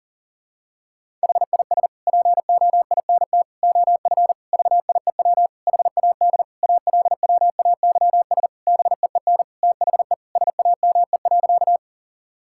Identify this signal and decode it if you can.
Morse code, heard as a keyed tone, about 37 wpm, 700 Hz. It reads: HIS POINT OF VIEW HAD ALWAYS BEEN THE SAME.
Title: Sentences from Top 1000 Words - Review Part 6 - 40wpm